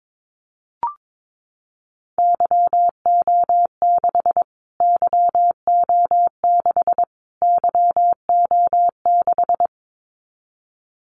Morse code, heard as a keyed tone, about 22 wpm, 700 Hz. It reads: YO6 YO6 YO6
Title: Callsign Prefixes for Top 25 DXCC - Repeated 3x - 22wpm